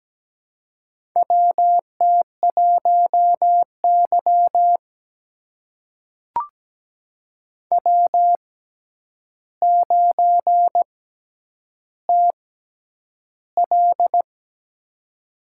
Morse code, heard as a keyed tone, about 17 wpm, spaced as for 9 wpm, 700 Hz. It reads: WT1Y W9TL
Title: USA Call Signs - 4 Characters - 6x Character Spacing - 17wpm